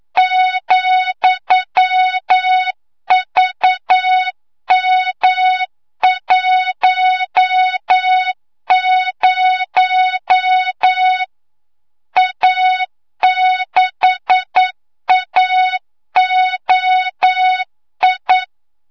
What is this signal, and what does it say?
Morse code, heard as a keyed tone, about 9 wpm, 3700 Hz, with noise. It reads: ,VM10 A6AOI
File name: Morze-HA2MN / hangp / 2 06 GYAKORLAT